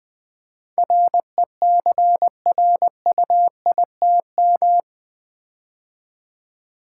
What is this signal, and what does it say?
Morse code, heard as a keyed tone, about 20 wpm, 700 Hz. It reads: RECRUITM
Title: Build A Word - 4 Letters At A Time - 11-Letter Words - 20wpm